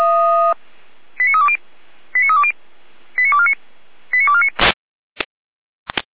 Five-tone selective call (ZVEI), heard as quick runs of five tones, tones of about 70 ms. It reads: 89320 89320 89249 89249